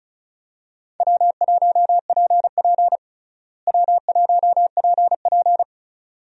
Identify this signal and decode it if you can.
Morse code, heard as a keyed tone, about 36 wpm, 700 Hz. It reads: W1PP W1PP
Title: USA Call Signs - 4 Characters - Repeated 2x - 35wpm